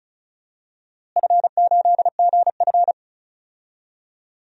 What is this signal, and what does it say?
Morse code, heard as a keyed tone, about 36 wpm, 700 Hz. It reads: F8GF
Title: DXCC-100 Call Signs - 4 Characters - Repeated 2x - 35wpm